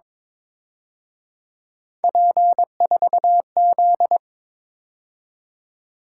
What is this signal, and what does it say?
Morse code, heard as a keyed tone, about 26 wpm, 700 Hz. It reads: P4Z
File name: Speed-Racing - Three Letter-Number Combinations - 22wpm